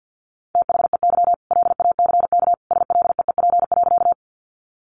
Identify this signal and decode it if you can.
Morse code, heard as a keyed tone, about 50 wpm, 700 Hz. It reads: THEY LACK SLEEP.
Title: Sentences from Top 1100 Words 50wpm